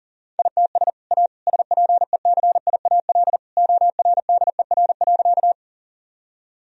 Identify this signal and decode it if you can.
Morse code, heard as a keyed tone, about 40 wpm, 700 Hz. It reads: ITS A SPECIAL ORDER.
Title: Sentences from Top 1000 Words - Review Part 3 - 40wpm